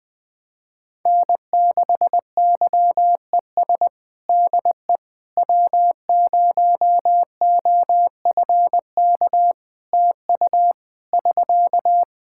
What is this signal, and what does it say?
Morse code, heard as a keyed tone, about 20 wpm, 700 Hz. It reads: N6YES DE W0OFK TU <SK>
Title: K1USN SST QSO Practice 20wpm